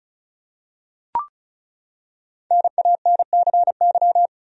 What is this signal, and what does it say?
Morse code, heard as a keyed tone, about 34 wpm, 700 Hz. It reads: NANCY
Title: USA Common Names - 35wpm